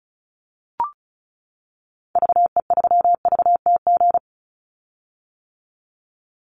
Morse code, heard as a keyed tone, about 34 wpm, 700 Hz. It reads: VE3VTG
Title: Call Signs - 35wpm